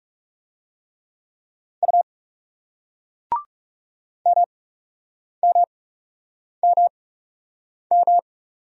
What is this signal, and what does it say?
Morse code, heard as a keyed tone, about 37 wpm, spaced as for 8 wpm, 700 Hz. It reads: U MMMM
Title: Speed-Racing Single Letter-Number 30wpm